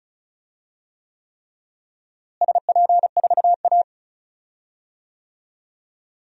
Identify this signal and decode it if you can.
Morse code, heard as a keyed tone, about 34 wpm, 700 Hz. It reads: SP4A
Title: DXCC-100 Call Signs - 4 Characters - Repeated 3x - 35wpm